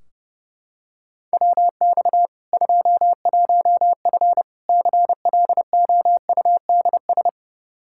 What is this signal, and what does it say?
Morse code, heard as a keyed tone, about 32 wpm, 700 Hz. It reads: WX 21F CLOUDS
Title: K1USN SST QSO Practice 30wpm